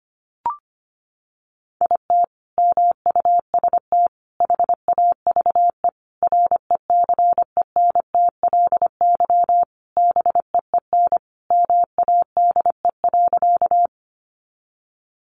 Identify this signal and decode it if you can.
Morse code, heard as a keyed tone, about 25 wpm, 700 Hz. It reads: IT MUST HAVE RECENTLY BEEN MADE.
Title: Sentences from Top 1200 Words 25wpm